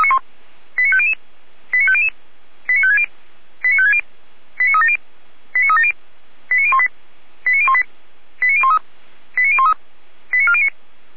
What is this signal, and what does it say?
Five-tone selective call (ZVEI), heard as quick runs of five tones, tones of about 70 ms. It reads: 89500 89500 89560 89560 89370 89370 89018 89018 89013 89013 89409
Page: ZVEI